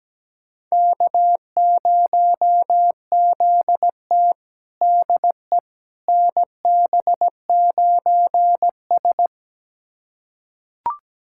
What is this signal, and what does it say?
Morse code, heard as a keyed tone, about 17 wpm, 700 Hz. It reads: K0ZT DE NB9S
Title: USA Callsign DE Callsign - 4 Characters - 17wpm